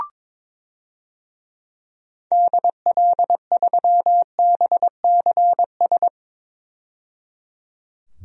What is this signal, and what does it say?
Morse code, heard as a keyed tone, about 21 wpm, 700 Hz. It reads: DL3BCS
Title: DXCC-100 Call Signs - 6 Characters - 22wpm